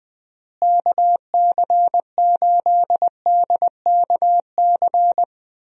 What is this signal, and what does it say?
Morse code, heard as a keyed tone, about 20 wpm, 700 Hz. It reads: KC8DKC
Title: USA Call Signs - 6 Characters - Repeated 3x - 20wpm